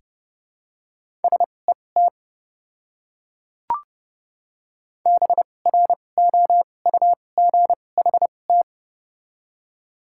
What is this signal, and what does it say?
Morse code, heard as a keyed tone, about 29 wpm, spaced as for 21 wpm, 700 Hz. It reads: SET BROUGHT
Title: 2x Character Spacing - Top 500 Words - 30wpm